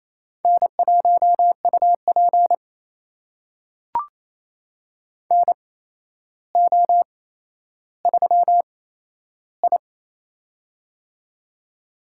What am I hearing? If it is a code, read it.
Morse code, heard as a keyed tone, about 28 wpm, spaced as for 12 wpm, 700 Hz. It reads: N1UP NO3I
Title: USA Call Signs - 4 Characters - 8x Character Spacing - 28wpm